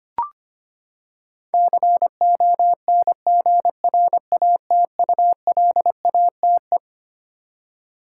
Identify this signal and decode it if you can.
Morse code, heard as a keyed tone, about 25 wpm, 700 Hz. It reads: CONGRATULATE
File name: Twelve Letter Words - 25wpm